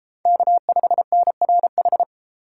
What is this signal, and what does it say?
Morse code, heard as a keyed tone, about 31 wpm, 700 Hz. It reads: K5NRH